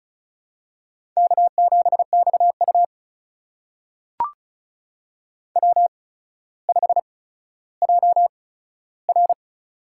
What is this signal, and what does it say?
Morse code, heard as a keyed tone, about 36 wpm, spaced as for 15 wpm, 700 Hz. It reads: K7XU W5JR
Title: USA Call Signs - 4 Characters - 8x Character Spacing - 35wpm